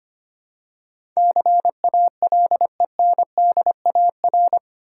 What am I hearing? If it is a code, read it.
Morse code, heard as a keyed tone, about 25 wpm, 700 Hz. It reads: CALENDAR
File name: practice / 25wpm / Easy Words with Hints - 25wpm